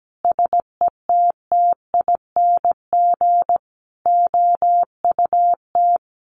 Morse code, heard as a keyed tone, about 17 wpm, 700 Hz. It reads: SETTING OUT
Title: Sentences from Top 1300 Words 17wpm